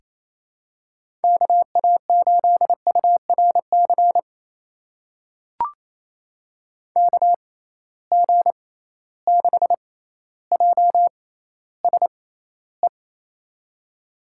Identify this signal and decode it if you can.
Morse code, heard as a keyed tone, about 28 wpm, spaced as for 14 wpm, 700 Hz. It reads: KA8URC KG6JSE
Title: USA Call Signs - 6 Characters - 6x Character Spacing - 28wpm